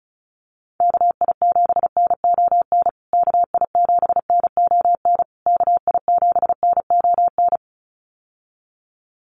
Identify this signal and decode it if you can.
Morse code, heard as a keyed tone, about 34 wpm, 700 Hz. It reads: KI7NON KI7NON KI7NON
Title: USA Call Signs Repeated 3x 35wpm